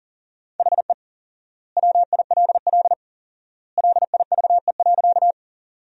Morse code, heard as a keyed tone, about 40 wpm, 700 Hz. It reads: HE WILL LIVE.
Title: 4x Word-Spacing - Sentences from Top 200 Words - 40wpm